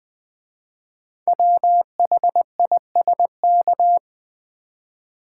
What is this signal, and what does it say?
Morse code, heard as a keyed tone, about 20 wpm, 700 Hz. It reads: WHISK
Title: Easy Words with Hints - 20wpm